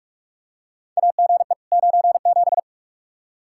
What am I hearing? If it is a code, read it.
Morse code, heard as a keyed tone, about 46 wpm, 700 Hz. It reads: AGE 97
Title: Practice QSO - Exchange 3 - More Info - 45wpm